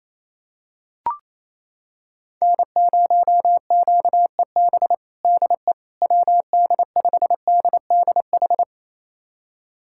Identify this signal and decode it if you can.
Morse code, heard as a keyed tone, about 28 wpm, 700 Hz. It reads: N0QEB DE WD5DDH
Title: USA Callsign DE Callsign 28wpm